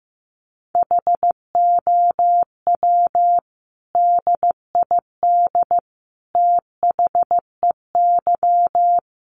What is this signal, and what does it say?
Morse code, heard as a keyed tone, about 15 wpm, 700 Hz. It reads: HOW DID THEY
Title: Sentences from Top 1100 Words 15wpm